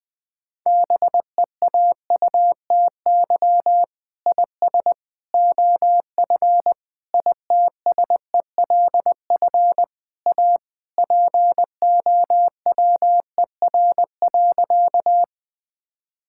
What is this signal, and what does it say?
Morse code, heard as a keyed tone, about 20 wpm, 700 Hz. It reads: BEAUTY IS OF ITSELF A POWER.
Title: Sentences from Top 1000 Words - Review Part 5 - 20wpm